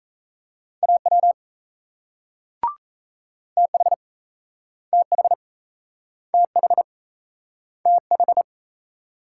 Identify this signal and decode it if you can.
Morse code, heard as a keyed tone, about 38 wpm, 700 Hz. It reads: AW TH TH TH TH